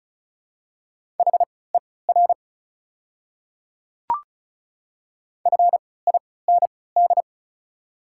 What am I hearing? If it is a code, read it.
Morse code, heard as a keyed tone, about 36 wpm, spaced as for 20 wpm, 700 Hz. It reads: HER FIND
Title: 3x Character Spacing - Top 100 Words - 35wpm